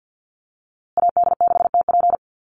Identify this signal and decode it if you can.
Morse code, heard as a keyed tone, about 50 wpm, 700 Hz. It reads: AD6TP
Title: USA Call Signs Repeated 3x 50wpm